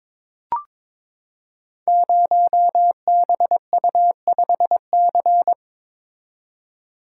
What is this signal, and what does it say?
Morse code, heard as a keyed tone, about 22 wpm, 700 Hz. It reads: 0BU5C
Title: Code Groups - 5 Letter:Numbers - 22wpm